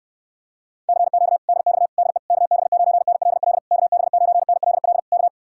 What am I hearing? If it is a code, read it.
Morse code, heard as a keyed tone, about 69 wpm, 700 Hz. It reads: CQ DX DE KB9NLL KB9NLL K